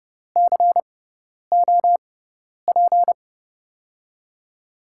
Morse code, heard as a keyed tone, about 32 wpm, 700 Hz. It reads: C O P
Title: USA Call Signs - 5 Characters - 6x Character Spacing - 30wpm